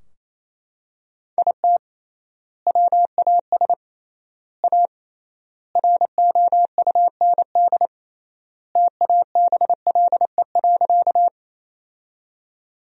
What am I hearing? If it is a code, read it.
Morse code, heard as a keyed tone, about 28 wpm, 700 Hz. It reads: IT WAS A ROUND TABLE.
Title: 3x Word-Spacing - Sentences from Top 400 Words - 28wpm